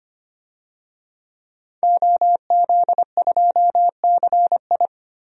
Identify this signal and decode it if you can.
Morse code, heard as a keyed tone, about 25 wpm, 700 Hz. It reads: OZ2CI
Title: DXCC-100 Call Signs - 5 Characters - 25wpm